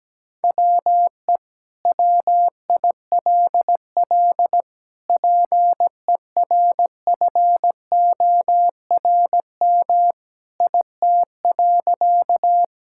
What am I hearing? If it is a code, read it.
Morse code, heard as a keyed tone, about 17 wpm, 700 Hz. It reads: WE WILL PERFORM IT.
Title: Sentences from Top 1000 Words - Review Part 5 - 17wpm